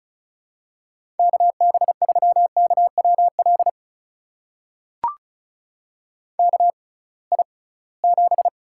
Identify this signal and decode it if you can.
Morse code, heard as a keyed tone, about 34 wpm, spaced as for 22 wpm, 700 Hz. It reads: KB3KWL KI7
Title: USA Call Signs - 6 Characters - 6x Character Spacing - 35wpm